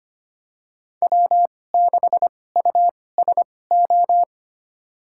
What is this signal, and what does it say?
Morse code, heard as a keyed tone, about 25 wpm, spaced as for 18 wpm, 700 Hz. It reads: W6USO
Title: USA Call Signs - 5 Characters - 2x Character Spacing - 25wpm